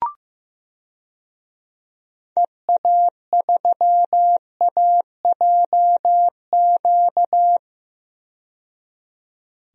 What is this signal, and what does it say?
Morse code, heard as a keyed tone, about 15 wpm, 700 Hz. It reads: EA3AJQ
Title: DXCC-100 Call Signs - 6 Characters - 15wpm